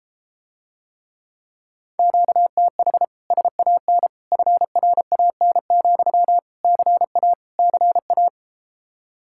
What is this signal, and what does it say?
Morse code, heard as a keyed tone, about 33 wpm, 700 Hz. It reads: QTH SAN FRAN, CA CA